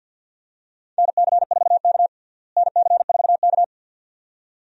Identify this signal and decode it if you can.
Morse code, heard as a keyed tone, about 52 wpm, 700 Hz. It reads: NC4K NC4K
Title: USA Call Signs - 4 Characters - Repeated 2x - 50wpm